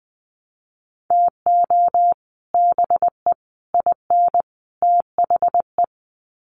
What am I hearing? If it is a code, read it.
Morse code, heard as a keyed tone, about 20 wpm, 700 Hz. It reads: TO BE IN THE
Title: Sets of 4 Words 20wpm